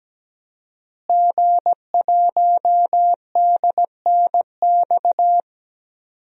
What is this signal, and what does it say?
Morse code, heard as a keyed tone, about 17 wpm, 700 Hz. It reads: G1DNX
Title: DXCC-100 Call Signs - 5 Characters - Repeated 3x - 17wpm